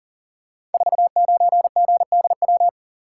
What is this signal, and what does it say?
Morse code, heard as a keyed tone, about 44 wpm, 700 Hz. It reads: 49GDW
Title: Code Groups - 5 Letter:Numbers - 40wpm